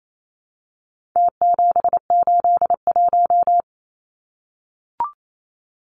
Morse code, heard as a keyed tone, about 28 wpm, 700 Hz. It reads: T781
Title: Numbers - Contest Style - 4-Digit 28wpm